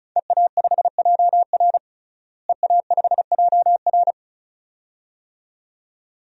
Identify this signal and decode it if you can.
Morse code, heard as a keyed tone, about 36 wpm, 700 Hz. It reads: EA5JR EA5JR
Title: DXCC-100 Call Signs - 5 Characters - Repeated 2x - 35wpm